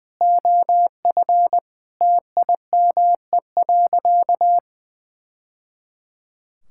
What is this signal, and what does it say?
Morse code, heard as a keyed tone, about 20 wpm, 700 Hz. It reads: OF TIME.
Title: Sentences from Top 1000 Words - Review Part 5 - 20wpm